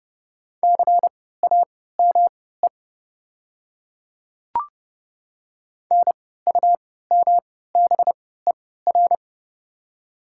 Morse code, heard as a keyed tone, about 30 wpm, spaced as for 17 wpm, 700 Hz. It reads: CAME NUMBER
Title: 3x Character Spacing - Top 200 Words - 30wpm